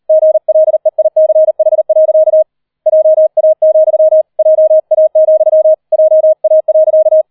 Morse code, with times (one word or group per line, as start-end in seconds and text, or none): 0.1-2.4 GLEICH.
2.9-4.2 JA,
4.4-5.7 JA,
5.9-7.2 JA.